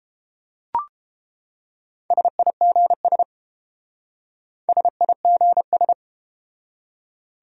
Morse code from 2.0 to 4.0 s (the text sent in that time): SIGS